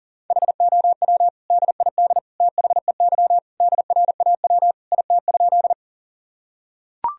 Morse code, heard as a keyed tone, about 40 wpm, 700 Hz. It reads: HOW DID THEY DRAW IT?
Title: Sentences from Top 1000 Words - Review Part 1 - 40wpm